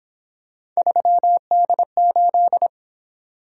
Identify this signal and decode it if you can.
Morse code, heard as a keyed tone, about 26 wpm, 700 Hz. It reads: 3D8
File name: Speed-Racing - Three Letter-Number Combinations - 22wpm